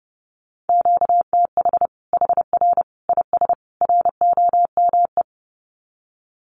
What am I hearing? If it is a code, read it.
Morse code, heard as a keyed tone, about 30 wpm, 700 Hz. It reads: QTH HR IS ROME
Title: Common QSO Phrases 30wpm